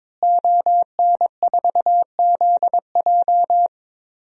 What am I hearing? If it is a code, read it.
Morse code, heard as a keyed tone, about 22 wpm, 700 Hz. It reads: ON4ZJ
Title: DXCC-100 Call Signs - 5 Characters - Repeated 2x - 22wpm